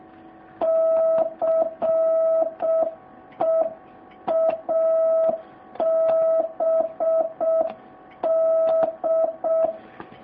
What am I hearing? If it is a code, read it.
Morse code, heard as a keyed tone, about 6 wpm, 650 Hz. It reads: CEABD